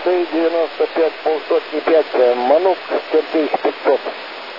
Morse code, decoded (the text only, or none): none